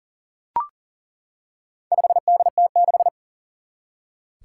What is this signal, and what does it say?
Morse code, heard as a keyed tone, about 40 wpm, 700 Hz. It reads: 5DT6
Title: Four Challenging Letter-Number Combinations 40wpm